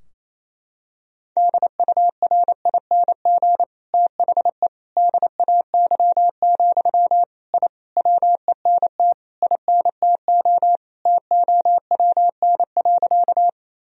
Morse code, heard as a keyed tone, about 28 wpm, 700 Hz. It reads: DURING THE DAY, I WENT INTO TOWN.
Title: Sentences from Top 1000 Words - Review Part 2 - 28wpm